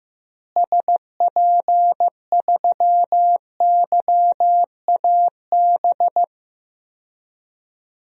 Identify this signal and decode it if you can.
Morse code, heard as a keyed tone, about 15 wpm, 700 Hz. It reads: SP3YAB
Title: DXCC-100 Call Signs - 6 Characters - Repeated 2x - 15wpm